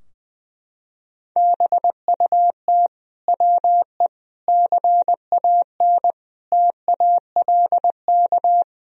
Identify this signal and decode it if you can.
Morse code, heard as a keyed tone, about 20 wpm, 700 Hz. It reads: BUT WE CAN TALK
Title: Sentences from Top 1000 Words - Review Part 4 - 20wpm